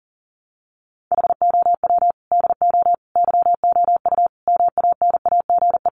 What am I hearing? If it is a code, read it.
Morse code, heard as a keyed tone, about 40 wpm, 700 Hz. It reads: HOW DO YOU MANAGE
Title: Sentences from Top 1400 Words 40wpm